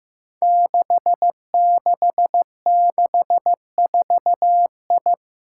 Morse code, heard as a keyed tone, about 15 wpm, 700 Hz. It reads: 6664I
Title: Five Challenging Letter-Number Combinations 15wpm